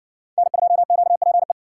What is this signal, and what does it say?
Morse code, heard as a keyed tone, about 57 wpm, 700 Hz. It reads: N2QPE